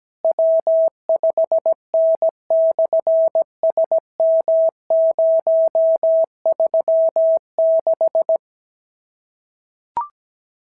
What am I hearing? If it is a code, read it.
Morse code, heard as a keyed tone, about 17 wpm, 650 Hz. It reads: W5N/SM036
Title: SOTA QSO Practice 17wpm